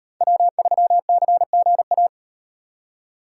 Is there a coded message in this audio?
Yes: Morse code from 0.2 to 2.1 s, heard as a keyed tone, about 37 wpm, 700 Hz. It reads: W3CGA